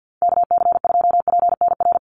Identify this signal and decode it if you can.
Morse code, heard as a keyed tone, about 50 wpm, 700 Hz. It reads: KC2PNR
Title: USA Call Signs 50wpm